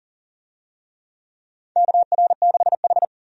Morse code, heard as a keyed tone, about 40 wpm, 700 Hz. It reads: KR6H
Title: USA Call Signs - 4 Characters - Repeated 3x - 40wpm